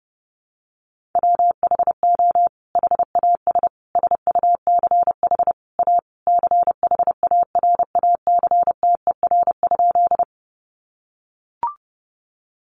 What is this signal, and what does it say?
Morse code, heard as a keyed tone, about 30 wpm, 700 Hz. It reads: WHO HAS SUCH A CHARACTER?
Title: Sentences from Top 1200 Words 30wpm